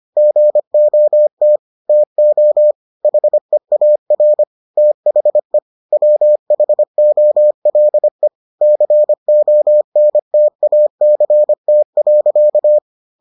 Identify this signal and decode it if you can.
Morse code, heard as a keyed tone, about 25 wpm, 600 Hz. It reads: GOT TO HEAR THE WHOLE CONTACT.